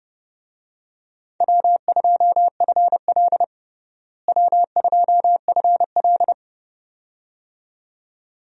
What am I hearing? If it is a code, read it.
Morse code, heard as a keyed tone, about 32 wpm, 700 Hz. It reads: W2FL W2FL
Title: USA Call Signs - 4 Characters - Repeated 2x - 30wpm